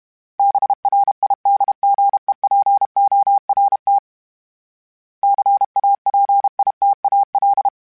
Morse code, heard as a keyed tone, about 32 wpm, 800 Hz. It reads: BRIDGEPORT CAPITAL
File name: fromSMs / FromSMs mp3s / Cities@30